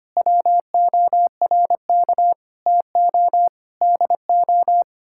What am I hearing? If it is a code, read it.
Morse code, heard as a keyed tone, about 25 wpm, 700 Hz. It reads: WORK TO DO
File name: Sets of 9 Words - 25wpm